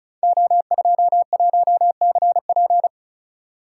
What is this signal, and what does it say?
Morse code, heard as a keyed tone, about 36 wpm, 700 Hz. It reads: O21CP
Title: Code Groups - 5 Letter:Numbers - 35wpm